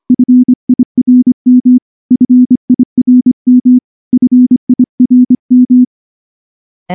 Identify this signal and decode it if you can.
Morse code, heard as a keyed tone, about 24 wpm, 250 Hz. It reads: FIRM FIRM FIRM
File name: morse-training-25WPM